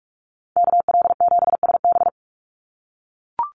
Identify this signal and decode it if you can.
Morse code, heard as a keyed tone, about 44 wpm, 700 Hz. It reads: KL7SB
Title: Call Signs - 45wpm